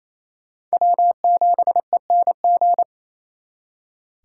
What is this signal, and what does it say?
Morse code, heard as a keyed tone, about 38 wpm, 700 Hz. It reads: W7ENG